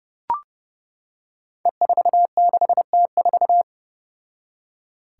Morse code, heard as a keyed tone, about 30 wpm, 700 Hz. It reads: E46T4
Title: Five Challenging Letter-Number Combinations 30wpm